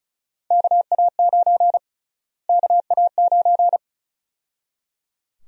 Morse code, heard as a keyed tone, about 36 wpm, 700 Hz. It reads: KA9 KA9
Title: USA Call Sign Prefixes - Repeated 2x - 35wpm